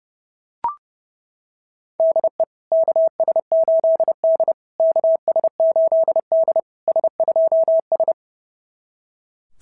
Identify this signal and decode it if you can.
Morse code, heard as a keyed tone, about 30 wpm, 690 Hz. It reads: DE KS8D KS8D S2S
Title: SOTA QSO Practice 30wpm